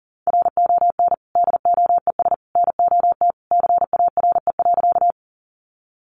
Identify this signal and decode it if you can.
Morse code, heard as a keyed tone, about 40 wpm, 700 Hz. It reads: RON DOES NOT CARE.
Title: Additional Easy Phrases 40wpm